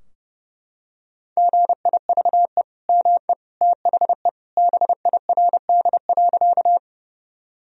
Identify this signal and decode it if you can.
Morse code, heard as a keyed tone, about 30 wpm, 700 Hz. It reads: GIVE ME THE BIRD.
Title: Sentences from Top 1000 Words - Review Part 2 - 30wpm